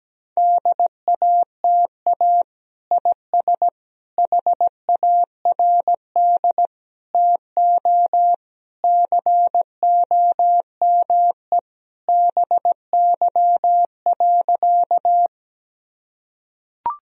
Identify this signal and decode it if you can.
Morse code, heard as a keyed tone, about 17 wpm, 700 Hz. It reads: DATA IS HARD TO COME BY.
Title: Sentences from Top 1000 Words - Review Part 3 - 17wpm